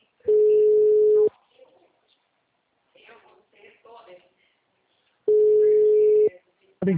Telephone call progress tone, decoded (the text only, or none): ringback tone